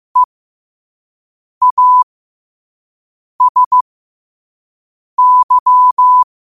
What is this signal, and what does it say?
Morse code, heard as a keyed tone, about 15 wpm, spaced as for 5 wpm, 1000 Hz. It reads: EASY
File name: qotdfiles / 5wpm